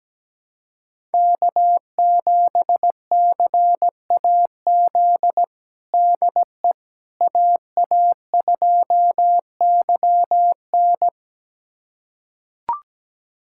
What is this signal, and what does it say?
Morse code, heard as a keyed tone, about 17 wpm, 700 Hz. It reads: K7CAZ DE AA2YN
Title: USA Callsign DE Callsign - 5 Characters - 17wpm